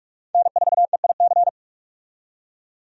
Morse code, heard as a keyed tone, about 46 wpm, 700 Hz. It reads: N4EIC
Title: USA Call Signs - 5 Characters - 45wpm